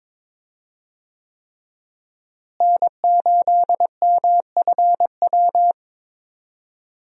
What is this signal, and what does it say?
Morse code, heard as a keyed tone, about 22 wpm, 700 Hz. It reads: N8MFW
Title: USA Call Signs - 5 Characters - Repeated 3x - 22wpm